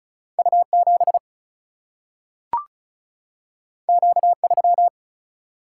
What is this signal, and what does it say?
Morse code, heard as a keyed tone, about 36 wpm, 700 Hz. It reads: U 7 Q 3
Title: Single-Letter Single-Number 35wpm